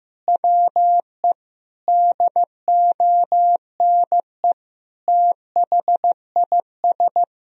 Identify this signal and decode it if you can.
Morse code, heard as a keyed tone, about 15 wpm, 700 Hz. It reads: WE DONE THIS
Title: Sentences from Top 1000 Words - Review Part 1 - 15wpm